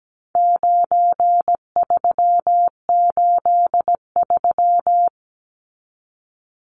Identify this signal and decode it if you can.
Morse code, heard as a keyed tone, about 17 wpm, 700 Hz. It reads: 9383
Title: Numbers - 4-Digit 17wpm